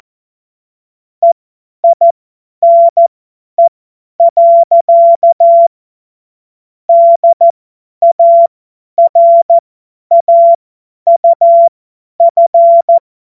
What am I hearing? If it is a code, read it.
Morse code, heard as a keyed tone, about 14 wpm, spaced as for 10 wpm, 680 Hz. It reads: EINE. DARAUF
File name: Morsekurs / Witze / Witz 13. Juli 2021, 50 F70